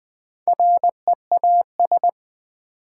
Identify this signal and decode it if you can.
Morse code, heard as a keyed tone, about 20 wpm, 700 Hz. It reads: REAS
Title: Build A Word - 2 Letters At A Time - 8-Letter Words - 20wpm